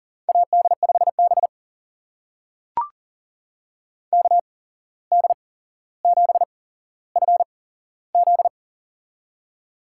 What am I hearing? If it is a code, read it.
Morse code, heard as a keyed tone, about 40 wpm, spaced as for 16 wpm, 700 Hz. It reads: AD5B KD7FZ
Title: USA Call Signs - 8x Character Spacing - 40wpm